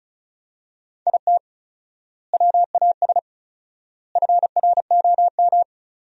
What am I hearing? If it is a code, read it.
Morse code, heard as a keyed tone, about 34 wpm, 700 Hz. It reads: IT WAS FROM
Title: 4x Word-Spacing - Sentences from Top 100 Words - 35wpm